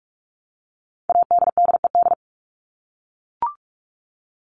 Morse code, heard as a keyed tone, about 44 wpm, 700 Hz. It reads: ADDED
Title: ED Suffix - Top 100 Words with Base Words 45wpm